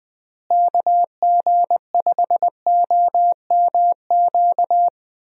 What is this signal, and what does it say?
Morse code, heard as a keyed tone, about 20 wpm, 700 Hz. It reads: KG5OMQ
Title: USA Call Signs - 6 Characters - Repeated 2x - 20wpm